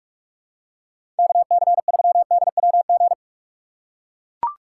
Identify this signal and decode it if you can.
Morse code, heard as a keyed tone, about 41 wpm, 700 Hz. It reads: KC3DWG
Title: USA Call Signs - 6 Characters - 3x Character Spacing - 45wpm